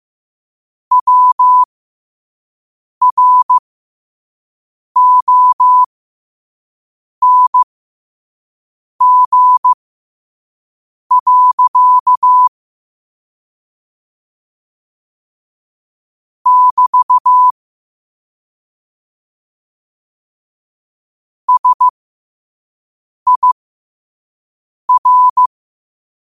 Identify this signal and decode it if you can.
Morse code, heard as a keyed tone, about 15 wpm, spaced as for 5 wpm, 1000 Hz. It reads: WRONG. <BT> SIR
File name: qotdfiles / 5wpm